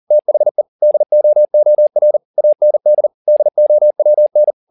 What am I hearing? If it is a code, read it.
Morse code, heard as a keyed tone, about 40 wpm, 600 Hz. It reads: THE DOOR AND DOWN
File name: the old man and the sea 2 40wpm